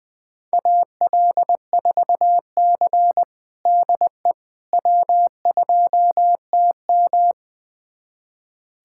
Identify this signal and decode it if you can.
Morse code, heard as a keyed tone, about 21 wpm, 700 Hz. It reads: AL4C DE W2TM
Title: USA Callsign DE Callsign - 4 Characters - 20wpm